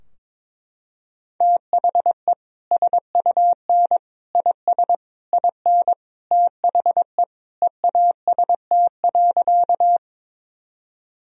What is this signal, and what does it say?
Morse code, heard as a keyed tone, about 22 wpm, 700 Hz. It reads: THE SUN IS IN THE EAST.
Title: Sentences from Top 1000 Words - Review Part 3 - 22wpm